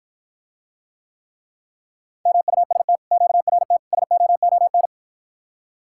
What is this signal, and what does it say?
Morse code, heard as a keyed tone, about 52 wpm, 700 Hz. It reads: MUST QRT SOON